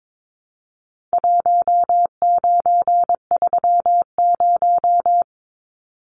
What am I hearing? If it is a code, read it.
Morse code, heard as a keyed tone, about 22 wpm, 700 Hz. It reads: 1930
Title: Numbers - Contest Style - 4-Digit 22wpm